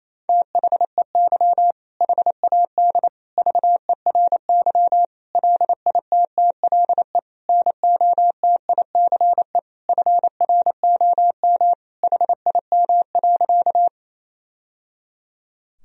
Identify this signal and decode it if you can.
Morse code, heard as a keyed tone, about 28 wpm, 700 Hz. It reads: THEY HAD VERY LITTLE NOTICE FROM HIM.
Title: Sentences from Top 1000 Words - Review Part 2 - 28wpm